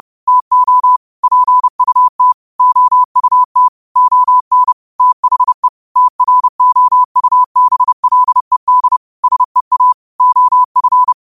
Morse code, heard as a keyed tone, about 30 wpm, 1000 Hz. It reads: TO PUT OUT ON THE TROUBLED SEA OF